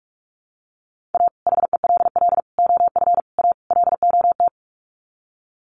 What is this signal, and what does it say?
Morse code, heard as a keyed tone, about 44 wpm, 700 Hz. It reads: A HELL OF A LOT
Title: Sets of 5 Words Encore 45wpm